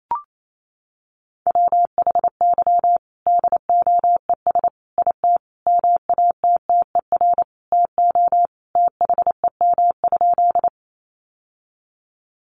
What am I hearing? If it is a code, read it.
Morse code, heard as a keyed tone, about 28 wpm, 700 Hz. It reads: WHY DOES IT MATTER TO THEM?
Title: Sentences from Top 700 Words 28wpm